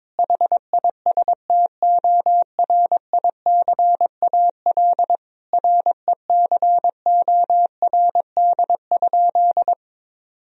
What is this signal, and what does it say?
Morse code, heard as a keyed tone, about 22 wpm, 700 Hz. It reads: HISTORICAL RECORD?
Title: Sentences from Top 1000 Words - Review Part 4 - 22wpm